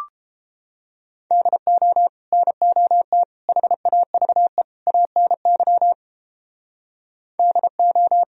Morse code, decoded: DO NOT HAVE ANY DO

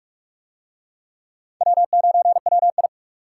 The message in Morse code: W9WI

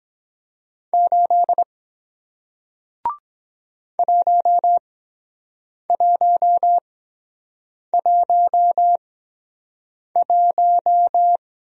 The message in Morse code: 8 1111

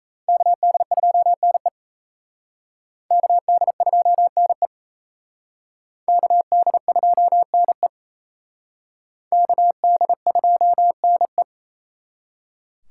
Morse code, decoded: KD2NE KD2NE KD2NE KD2NE

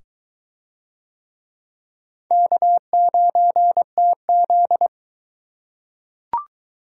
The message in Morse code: K9TZ

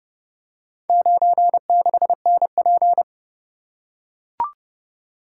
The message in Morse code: 96NP

